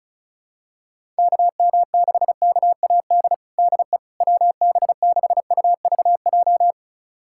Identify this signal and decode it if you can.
Morse code, heard as a keyed tone, about 36 wpm, 700 Hz. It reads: KM6KAD DE WB6UVJ
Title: USA Callsign DE Callsign - 6 Characters - 35wpm